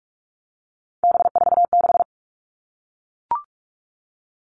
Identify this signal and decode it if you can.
Morse code, heard as a keyed tone, about 44 wpm, 700 Hz. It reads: B46
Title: All Cut Numbers - 3-Digit 45wpm